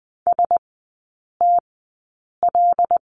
Morse code, heard as a keyed tone, about 20 wpm, spaced as for 8 wpm, 700 Hz. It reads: STL
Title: 074 Lesson 13e - Letter L - Three Characters 20wpm